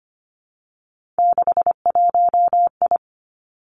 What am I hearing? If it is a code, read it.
Morse code, heard as a keyed tone, about 25 wpm, 700 Hz. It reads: 61I
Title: Three Challenging Letter-Number Combinations 25wpm